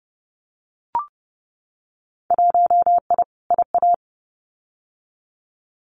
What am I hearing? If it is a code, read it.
Morse code, heard as a keyed tone, about 30 wpm, 700 Hz. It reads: 1I IA
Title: Winter Field Day Exchanges 30wpm